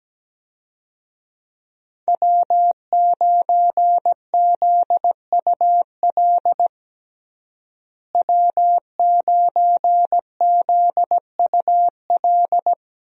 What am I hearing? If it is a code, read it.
Morse code, heard as a keyed tone, about 17 wpm, 700 Hz. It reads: W9ZUL W9ZUL
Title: USA Call Signs - 5 Characters - Repeated 3x - 17wpm